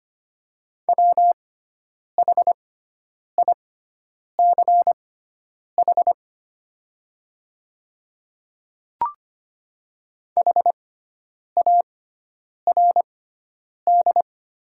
Morse code, heard as a keyed tone, about 25 wpm, spaced as for 9 wpm, 700 Hz. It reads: WHICH HARD
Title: 6x Character Spacing - Top 300 Words - 25wpm